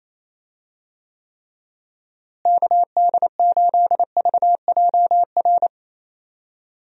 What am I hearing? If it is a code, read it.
Morse code, heard as a keyed tone, about 27 wpm, 700 Hz. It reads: KD8VJR
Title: USA Call Signs - 6 Characters - Repeated 2x - 28wpm